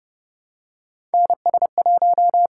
Morse code, heard as a keyed tone, about 31 wpm, 700 Hz. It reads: NS1
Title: USA Call Signs - 5 Characters - 8x Character Spacing - 30wpm